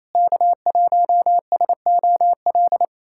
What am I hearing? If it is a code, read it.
Morse code, heard as a keyed tone, about 28 wpm, 700 Hz. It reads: K1SOL
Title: USA Callsign DE Callsign - 5 Characters - 28wpm